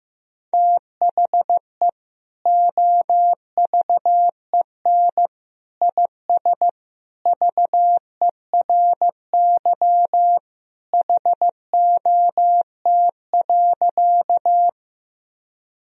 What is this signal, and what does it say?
Morse code, heard as a keyed tone, about 15 wpm, 700 Hz. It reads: THE OVEN IS VERY HOT.
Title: Sentences from Top 1000 Words - Review Part 4 - 15wpm